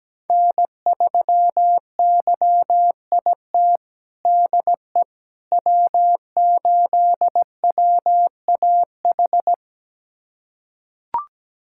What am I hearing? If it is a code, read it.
Morse code, heard as a keyed tone, about 17 wpm, 700 Hz. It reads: N3YIT DE W8WAH
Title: USA Callsign DE Callsign - 5 Characters - 17wpm